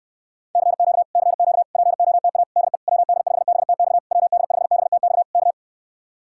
Morse code, heard as a keyed tone, about 69 wpm, 700 Hz. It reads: CQ CQ POTA DE WD5BTX WD5BTX K